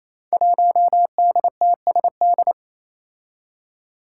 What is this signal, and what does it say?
Morse code, heard as a keyed tone, about 28 wpm, 700 Hz. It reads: 1DTSD
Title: Five Challenging Letter-Number Combinations 28wpm